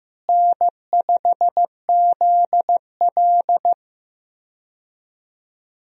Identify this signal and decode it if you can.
Morse code, heard as a keyed tone, about 15 wpm, 700 Hz. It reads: N5ZL
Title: USA Call Signs - 4 Characters - Repeated 2x - 15wpm